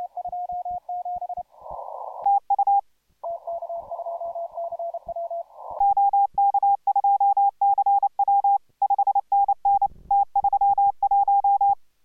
Morse code, heard as a keyed tone, about 29 wpm, 800 Hz. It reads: TU OK2CW 5NN T31